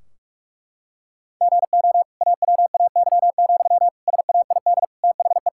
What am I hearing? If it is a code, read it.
Morse code, heard as a keyed tone, about 42 wpm, 700 Hz. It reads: GO AWAY, SAID THE
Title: Sentences from Top 1600 Words 45wpm